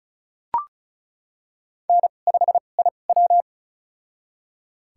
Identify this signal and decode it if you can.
Morse code, heard as a keyed tone, about 36 wpm, spaced as for 25 wpm, 700 Hz. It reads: N5IW
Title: USA Call Signs - 4 Characters - 2x Character Spacing - 35wpm